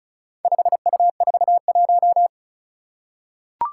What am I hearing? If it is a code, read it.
Morse code, heard as a keyed tone, about 34 wpm, 700 Hz. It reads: 5U41
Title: Four Challenging Letter-Number Combinations 35wpm